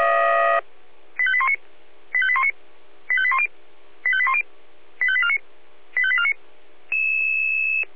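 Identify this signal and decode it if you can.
Five-tone selective call (ZVEI), heard as quick runs of five tones, tones of about 70 ms. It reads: 85819 85819 85810 85810 85849 85849
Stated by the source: ZVEI five-tone selective call received by SDR.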